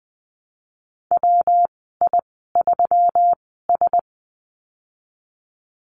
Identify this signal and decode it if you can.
Morse code, heard as a keyed tone, about 20 wpm, spaced as for 15 wpm, 700 Hz. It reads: WI3S
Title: 147 Lesson 24g - Letter y - Call Signs 20wpm